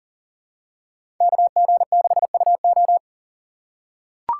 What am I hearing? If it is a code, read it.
Morse code, heard as a keyed tone, about 40 wpm, 700 Hz. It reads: KG6UO
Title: USA Call Signs - 5 Characters - 2x Character Spacing - 40wpm